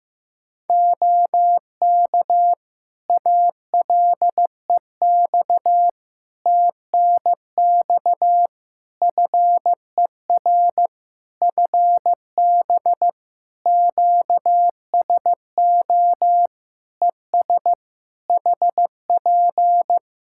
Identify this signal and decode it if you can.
Morse code, heard as a keyed tone, about 15 wpm, 700 Hz. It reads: OK ALEX TNX FER FB QSO ES HP
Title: Practice QSO - Exchange 4 - Ending - 15wpm